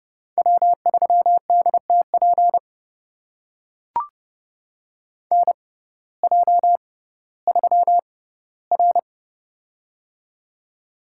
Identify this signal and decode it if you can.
Morse code, heard as a keyed tone, about 30 wpm, spaced as for 16 wpm, 700 Hz. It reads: W3DTP NJ3R
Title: USA Call Signs - 6x Character Spacing - 30wpm